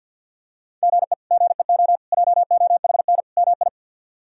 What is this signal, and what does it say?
Morse code, heard as a keyed tone, about 53 wpm, 690 Hz. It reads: GE GEO JOHN MI
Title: K1USN SST QSO Practice - 50wpm